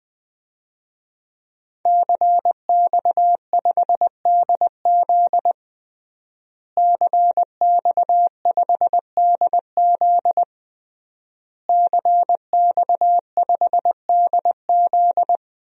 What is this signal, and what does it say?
Morse code, heard as a keyed tone, about 20 wpm, 700 Hz. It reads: CX5DZ CX5DZ CX5DZ
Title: DXCC-100 Call Signs - 5 Characters - Repeated 3x - 20wpm